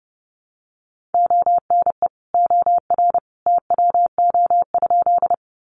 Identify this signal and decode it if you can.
Morse code, heard as a keyed tone, about 30 wpm, 700 Hz. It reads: ONE OR TWO?
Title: Sentences from Top 100 Words 30wpm